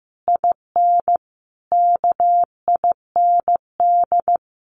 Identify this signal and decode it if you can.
Morse code, heard as a keyed tone, about 15 wpm, 700 Hz. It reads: IN KIND
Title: Sentences from Top 1500 Words 15wpm